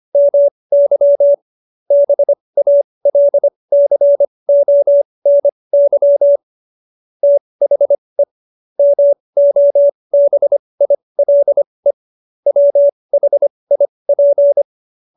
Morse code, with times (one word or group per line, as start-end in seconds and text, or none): 0.2-1.3 MY
1.9-6.4 BALCONY
7.2-8.2 THE
8.8-11.9 MOBILE
12.5-14.6 WHIP